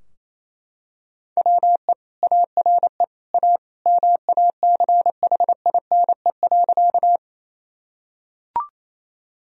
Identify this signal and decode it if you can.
Morse code, heard as a keyed tone, about 27 wpm, 700 Hz. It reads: WE ARE A MACHINE.
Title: Sentences from Top 1000 Words - Review Part 2 - 28wpm